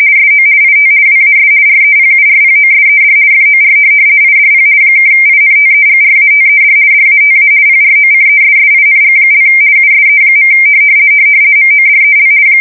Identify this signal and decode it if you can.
RTTY, about 50 baud. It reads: SO IT WAS REPURPOSED FOR IDP (INTEGRATED DATA PROCESSING) USE. MODEL
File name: fsk170-50B / teletype-corp